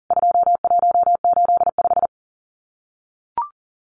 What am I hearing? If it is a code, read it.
Morse code, heard as a keyed tone, about 40 wpm, 700 Hz. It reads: 2185
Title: CWT Exchange 40wpm